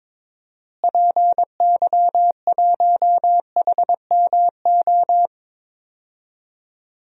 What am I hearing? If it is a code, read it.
Morse code, heard as a keyed tone, about 22 wpm, 700 Hz. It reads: PY1HMO
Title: DXCC-100 Call Signs - 6 Characters - Repeated 2x - 22wpm